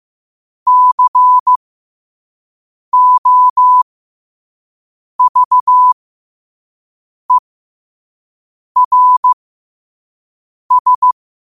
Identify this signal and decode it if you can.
Morse code, heard as a keyed tone, about 15 wpm, spaced as for 5 wpm, 1000 Hz. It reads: COVERS
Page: qotd-at-5-wpm